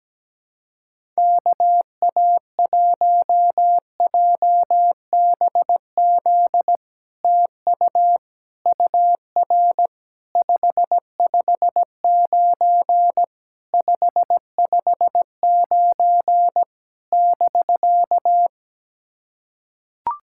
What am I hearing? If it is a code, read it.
Morse code, heard as a keyed tone, about 17 wpm, 700 Hz. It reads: KA1JBZ TU UR 559 559 <BK>
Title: SOTA QSO Practice 17wpm